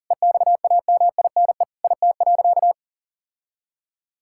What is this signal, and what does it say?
Morse code, heard as a keyed tone, about 40 wpm, 700 Hz. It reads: EXAMINE IT.
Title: Sentences from Top 1600 Words 40wpm